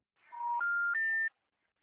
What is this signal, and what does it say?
Special information tone.